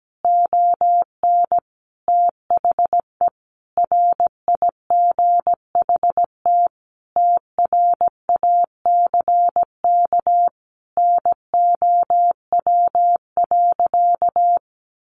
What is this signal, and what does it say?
Morse code, heard as a keyed tone, about 17 wpm, 700 Hz. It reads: ON THE RIGHT TRACK NOW.
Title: Sentences from Top 1100 Words 17wpm